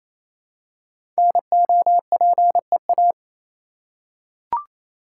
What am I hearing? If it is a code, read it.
Morse code, heard as a keyed tone, about 28 wpm, 700 Hz. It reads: NOPEA
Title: Code Groups - 5 Letters - 28wpm